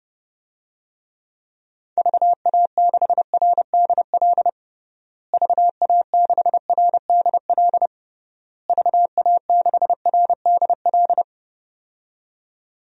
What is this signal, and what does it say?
Morse code, heard as a keyed tone, about 30 wpm, 700 Hz. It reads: VA6RDL VA6RDL VA6RDL
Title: DXCC-100 Call Signs - 6 Characters - Repeated 3x - 30wpm